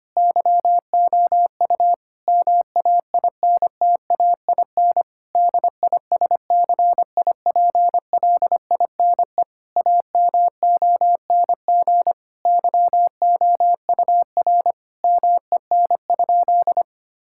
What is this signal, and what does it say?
Morse code, heard as a keyed tone, about 25 wpm, 700 Hz. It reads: YOU MAINTAIN DISCIPLINE AMONG YOUR MEN?
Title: Sentences from Top 1000 Words - Review Part 6 - 25wpm